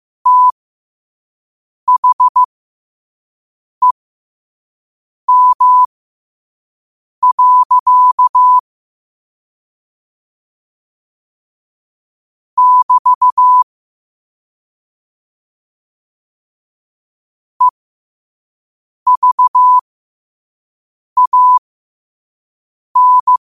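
Morse code, heard as a keyed tone, about 15 wpm, spaced as for 5 wpm, 1000 Hz. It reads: THEM. <BT> EVAN
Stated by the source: Morse code sent at 5 wpm.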